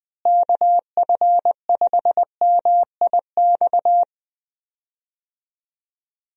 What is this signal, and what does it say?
Morse code, heard as a keyed tone, about 20 wpm, 700 Hz. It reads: KF5MIX
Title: USA Call Signs - 6 Characters - Repeated 3x - 20wpm